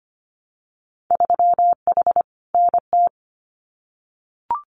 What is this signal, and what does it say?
Morse code, heard as a keyed tone, about 25 wpm, 700 Hz. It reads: 3H NT
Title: Winter Field Day Exchanges Encore 25wpm